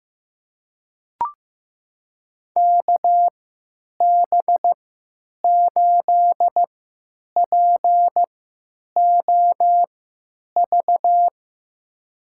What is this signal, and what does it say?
Morse code, heard as a keyed tone, about 15 wpm, spaced as for 9 wpm, 700 Hz. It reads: KB8POV
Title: USA Call Signs - 3x Character Spacing - 15wpm